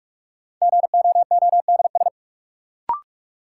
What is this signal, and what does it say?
Morse code, heard as a keyed tone, about 46 wpm, 700 Hz. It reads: GOODS
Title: Sentences from Top 1000 Words - Review Part 6 - New Word Drill - 45wpm